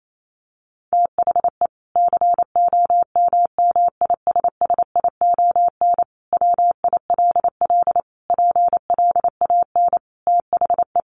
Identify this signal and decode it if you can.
Morse code, heard as a keyed tone, about 28 wpm, 700 Hz. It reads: THE COMMISSION WILL PLAN THE
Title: Sentences from Top 1400 Words 28wpm